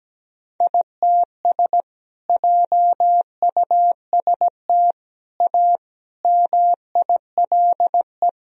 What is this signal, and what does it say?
Morse code, heard as a keyed tone, about 17 wpm, 700 Hz. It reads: ITS JUST A MILE
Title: Sentences from Top 1000 Words - Review Part 1 - 17wpm